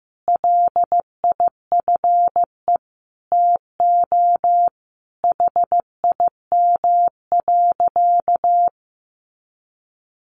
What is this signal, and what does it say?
Morse code, heard as a keyed tone, about 15 wpm, 700 Hz. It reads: LIFE TO HIM.
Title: Sentences from Top 300 Words 15wpm